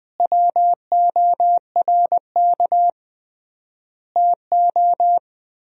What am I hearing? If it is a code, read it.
Morse code, heard as a keyed tone, about 20 wpm, 700 Hz. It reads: WORK TO
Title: 3x Word-Spacing - Sentences from Top 200 Words - 20wpm